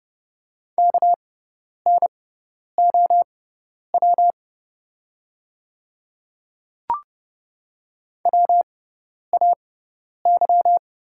Morse code, heard as a keyed tone, about 29 wpm, spaced as for 10 wpm, 690 Hz. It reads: KNOW WAY
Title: 6x Character Spacing - Top 100 Words - 30wpm